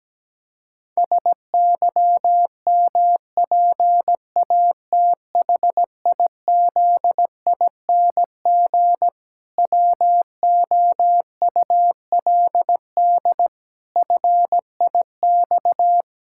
Morse code, heard as a keyed tone, about 17 wpm, 700 Hz. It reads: SYMPATHIZING WOULD FIX
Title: Pangram Challenge v2 - 17wpm